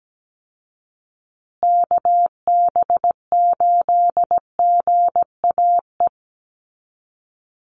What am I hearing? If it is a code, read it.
Morse code, heard as a keyed tone, about 17 wpm, 700 Hz. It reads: KB8GAE
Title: Call Signs Repeated 2x - 17wpm